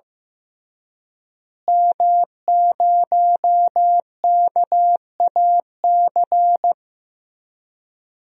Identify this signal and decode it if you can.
Morse code, heard as a keyed tone, about 15 wpm, 700 Hz. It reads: M0KAC
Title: DXCC-100 Call Signs - 5 Characters - Repeated 3x - 15wpm